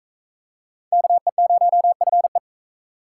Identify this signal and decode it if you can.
Morse code, heard as a keyed tone, about 41 wpm, 700 Hz. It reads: KE0FE